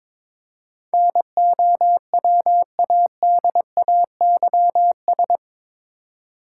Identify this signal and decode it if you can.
Morse code, heard as a keyed tone, about 22 wpm, 700 Hz. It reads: NOWADAYS
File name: practice / 22wpm / Eight Letter Words - 22wpm